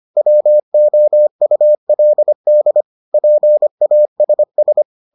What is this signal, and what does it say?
Morse code, heard as a keyed tone, about 25 wpm, 600 Hz. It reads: WOULD PASS